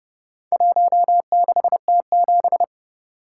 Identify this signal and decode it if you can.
Morse code, heard as a keyed tone, about 28 wpm, 700 Hz. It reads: 16T7